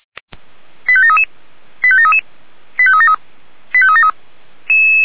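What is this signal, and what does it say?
Five-tone selective call (ZVEI), heard as quick runs of five tones, tones of about 80 ms. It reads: 85733 85733 85373 85373
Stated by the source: ZVEI five-tone selective call received on an SDR.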